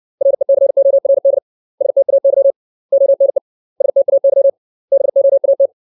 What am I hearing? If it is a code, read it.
Morse code, heard as a keyed tone, about 60 wpm, 550 Hz. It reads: RECORD STAY ONE STAY BOAT
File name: 632 words / 632 Words 60WPM